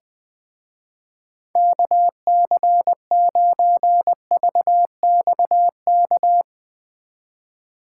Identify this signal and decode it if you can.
Morse code, heard as a keyed tone, about 20 wpm, 700 Hz. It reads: KC9VXK
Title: Speed Racing - USA Call Signs 17wpm